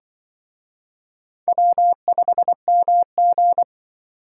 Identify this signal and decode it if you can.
Morse code, heard as a keyed tone, about 24 wpm, 700 Hz. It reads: W5MG
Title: Speed Racing - USA 1x2 Call Signs 20wpm